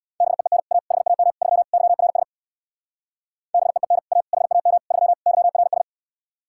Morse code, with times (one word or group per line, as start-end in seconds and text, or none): 0.2-0.6 BEEN
0.7-0.8 A
0.9-1.3 HAM
1.4-1.6 3
1.7-2.2 YRS
3.5-4.0 BEEN
4.1-4.2 A
4.3-4.8 HAM
4.9-5.1 3
5.3-5.8 YRS